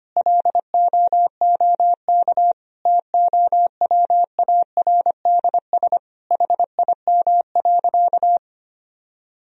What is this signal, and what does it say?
Morse code, heard as a keyed tone, about 25 wpm, 700 Hz. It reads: LOOK TOWARDS HIM.
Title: Sentences from Top 1000 Words - Review Part 2 - 25wpm